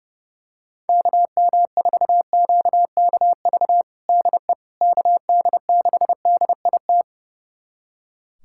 Morse code, heard as a keyed tone, about 30 wpm, 700 Hz. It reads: KM4QKV DE KD6DIT
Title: USA Callsign DE Callsign - 6 Characters - 30wpm